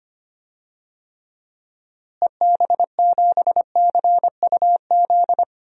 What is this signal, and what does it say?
Morse code, heard as a keyed tone, about 25 wpm, 700 Hz. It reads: EB7CUZ
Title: DXCC-100 Call Signs - 6 Characters - Repeated 2x - 25wpm